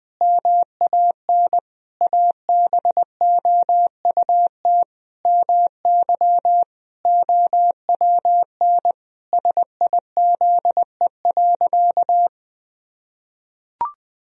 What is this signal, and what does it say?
Morse code, heard as a keyed tone, about 21 wpm, 700 Hz. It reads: MAN ABOUT MY OWN SIZE.
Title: Sentences from Top 1000 Words - Review Part 2 - 20wpm